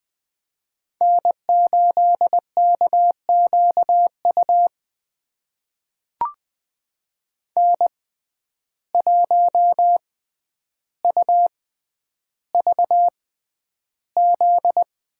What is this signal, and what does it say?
Morse code, heard as a keyed tone, about 20 wpm, spaced as for 10 wpm, 700 Hz. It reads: N8KQU N1UVZ
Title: USA Call Signs - 5 Characters - 6x Character Spacing - 20wpm